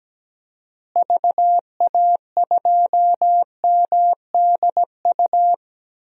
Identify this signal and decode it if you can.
Morse code, heard as a keyed tone, about 17 wpm, 700 Hz. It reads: VA2MDU
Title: DXCC-100 Call Signs - 6 Characters - Repeated 3x - 17wpm